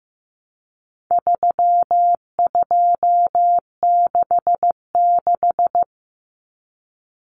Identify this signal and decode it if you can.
Morse code, heard as a keyed tone, about 15 wpm, 700 Hz. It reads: 3266
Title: Numbers - 4-Digit 15wpm